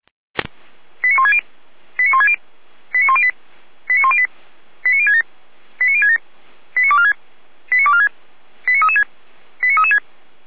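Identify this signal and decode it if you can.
Five-tone selective call (ZVEI), heard as quick runs of five tones, tones of about 70 ms. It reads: 89160 89160 89108 89108 89067 89067 89246 89246 89336 89336
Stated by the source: Received on SDR, ZVEI five-tone code 89078.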